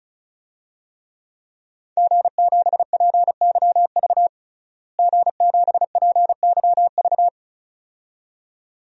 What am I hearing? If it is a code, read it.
Morse code, heard as a keyed tone, about 36 wpm, 700 Hz. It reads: G7PYV G7PYV
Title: DXCC-100 Call Signs - 5 Characters - Repeated 2x - 35wpm